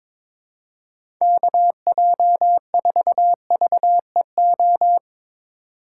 Morse code, heard as a keyed tone, about 22 wpm, 700 Hz. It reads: KJ4VEO
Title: USA Call Signs - 6 Characters - 2x Character Spacing - 22wpm